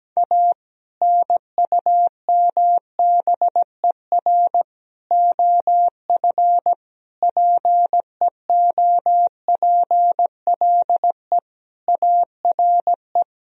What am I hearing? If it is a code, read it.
Morse code, heard as a keyed tone, about 17 wpm, 700 Hz. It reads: A NUMBER OF PEOPLE ARE
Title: Sentences from Top 1000 Words - Review Part 1 - 17wpm